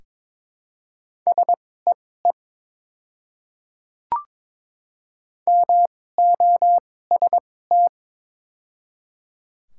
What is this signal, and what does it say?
Morse code, heard as a keyed tone, about 22 wpm, spaced as for 16 wpm, 700 Hz. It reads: SEE MOST
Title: 2x Character Spacing - Top 200 Words - 22wpm